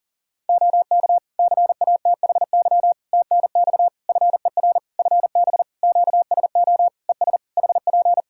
Morse code, heard as a keyed tone, about 40 wpm, 700 Hz. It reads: OK CATHY TNX FER FB QSO ES HP
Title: K1USN SST QSO Practice 40wpm